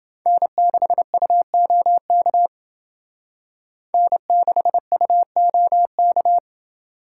Morse code, heard as a keyed tone, about 28 wpm, 700 Hz. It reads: N6UOK N6UOK